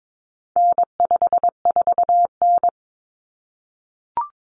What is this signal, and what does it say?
Morse code, heard as a keyed tone, about 22 wpm, 700 Hz. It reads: N54N
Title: Common Cut Numbers - 4-Digit 22wpm